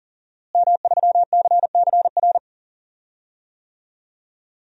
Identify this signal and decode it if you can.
Morse code, heard as a keyed tone, about 41 wpm, 700 Hz. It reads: M3CCR
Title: DXCC-100 Call Signs - 5 Characters - Repeated 2x - 40wpm